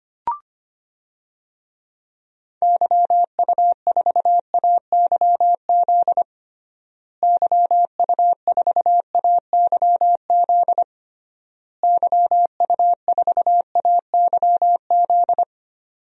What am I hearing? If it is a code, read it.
Morse code, heard as a keyed tone, about 25 wpm, 700 Hz. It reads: YU4AYZ YU4AYZ YU4AYZ
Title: DXCC-100 Call Signs - 6 Characters - Repeated 3x - 25wpm